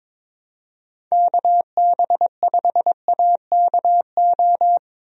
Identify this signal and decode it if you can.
Morse code, heard as a keyed tone, about 22 wpm, 700 Hz. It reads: KB5AKO
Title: USA Call Signs - 6 Characters - Repeated 2x - 22wpm